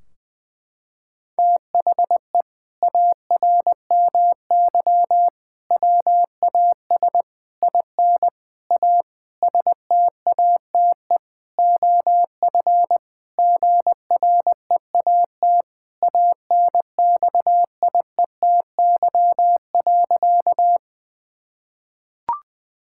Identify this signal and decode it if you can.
Morse code, heard as a keyed tone, about 20 wpm, 700 Hz. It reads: THE ARMY WAS IN A STATE OF GREAT ANXIETY.
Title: Sentences from Top 1600 Words 20wpm